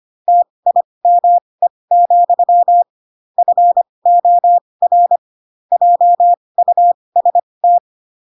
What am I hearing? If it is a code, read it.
Morse code, heard as a keyed tone, about 25 wpm, spaced as for 20 wpm, 700 Hz. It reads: TIME, FOR JUST
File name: Pinocchio / Pinocchio ch4 15 25 ik1hge